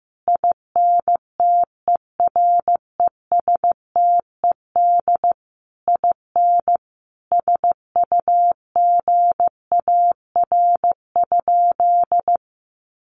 Morse code, heard as a keyed tone, about 15 wpm, 700 Hz. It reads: INTERESTED IN SUGAR?
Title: Sentences from Top 1200 Words 15wpm